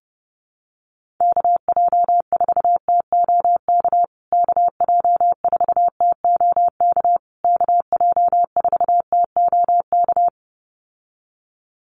Morse code, heard as a keyed tone, about 30 wpm, 700 Hz. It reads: KJ4TOK KJ4TOK KJ4TOK
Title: USA Call Signs Repeated 3x 30wpm